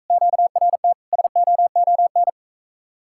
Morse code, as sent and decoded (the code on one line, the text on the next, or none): --.- .-. - / ... --- --- -.
QRT SOON